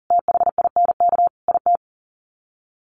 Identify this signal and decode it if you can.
Morse code, heard as a keyed tone, about 40 wpm, 700 Hz. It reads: THINK IT
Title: Sets of 4 Words 40wpm